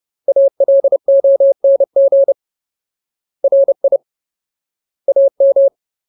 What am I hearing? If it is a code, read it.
Morse code, heard as a keyed tone, about 30 wpm, 550 Hz. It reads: ALONG RI AM
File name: Ultimate Word List / 30WPM Ultimate